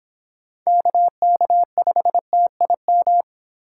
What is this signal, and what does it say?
Morse code, heard as a keyed tone, about 24 wpm, 700 Hz. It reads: KK5TIM